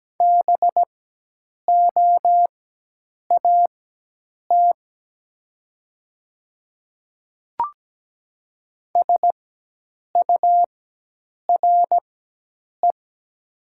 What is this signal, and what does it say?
Morse code, heard as a keyed tone, about 17 wpm, spaced as for 8 wpm, 700 Hz. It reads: BOAT SURE
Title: 4x Character Spacing - Top 500 Words - 17wpm